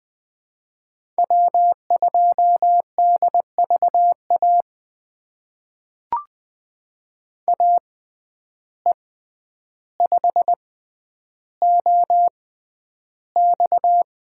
Morse code, heard as a keyed tone, about 20 wpm, spaced as for 10 wpm, 700 Hz. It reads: W2DVA AE5OX
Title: USA Call Signs - 5 Characters - 6x Character Spacing - 20wpm